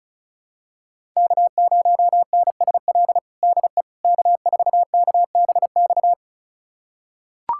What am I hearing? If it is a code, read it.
Morse code, heard as a keyed tone, about 34 wpm, 700 Hz. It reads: K0NSL DE K4KBX
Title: USA Callsign DE Callsign - 5 Characters - 35wpm